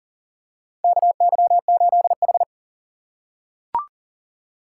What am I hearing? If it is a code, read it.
Morse code, heard as a keyed tone, about 40 wpm, 700 Hz. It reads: KY8H
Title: USA Call Signs - 4 Characters - 6x Character Spacing - 40wpm